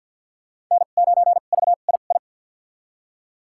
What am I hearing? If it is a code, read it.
Morse code, heard as a keyed tone, about 53 wpm, spaced as for 37 wpm, 690 Hz. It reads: N9VII
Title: USA Call Signs - 2x Character Spacing - 50wpm